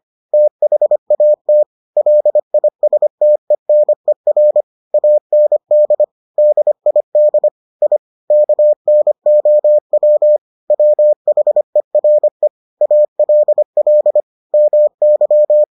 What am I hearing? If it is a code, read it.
Morse code, heard as a keyed tone, about 25 wpm, 600 Hz. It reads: THAT LISTENER AND DID I KNOW WHERE ALL MY